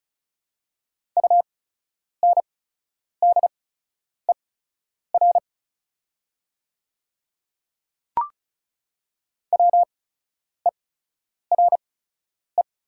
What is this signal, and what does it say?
Morse code, heard as a keyed tone, about 36 wpm, spaced as for 10 wpm, 700 Hz. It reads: UNDER WERE
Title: 8x Character Spacing - Top 500 Words - 35wpm